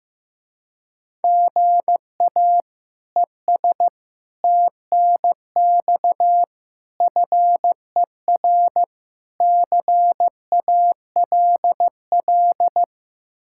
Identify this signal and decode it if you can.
Morse code, heard as a keyed tone, about 15 wpm, 700 Hz. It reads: GA ES TNX FER CALL
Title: Speed-Racing Common QSO Phrases Encore - 15wpm